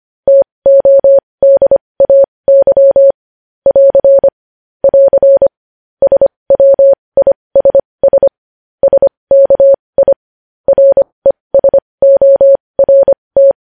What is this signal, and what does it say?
Morse code, heard as a keyed tone, about 25 wpm, spaced as for 20 wpm, 550 Hz. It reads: TODAY <AR> <AR> SWISS SKI RESORT
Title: News Headlines at 20